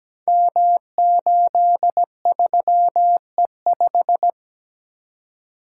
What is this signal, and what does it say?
Morse code, heard as a keyed tone, about 17 wpm, 700 Hz. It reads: M83E5
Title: Code Groups - 5 Letter:Numbers - 17wpm